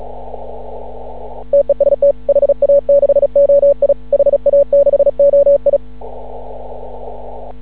Morse code, heard as a keyed tone, about 36 wpm, 560 Hz. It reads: TEST HA6OI HA6OI
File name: CW / ha6oi 2